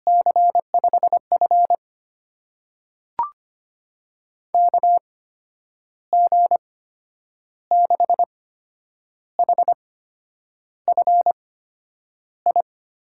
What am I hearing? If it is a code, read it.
Morse code, heard as a keyed tone, about 25 wpm, spaced as for 9 wpm, 700 Hz. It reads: C5F KG6HFI
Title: USA Call Signs - 8x Character Spacing - 25wpm